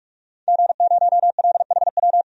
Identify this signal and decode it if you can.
Morse code, heard as a keyed tone, about 44 wpm, 700 Hz. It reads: G0LHW
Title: DXCC-100 Call Signs - 5 Characters - 45wpm